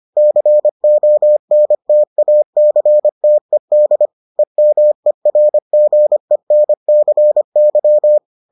Morse code, heard as a keyed tone, about 25 wpm, 600 Hz. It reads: CONTACTED EMERGENCY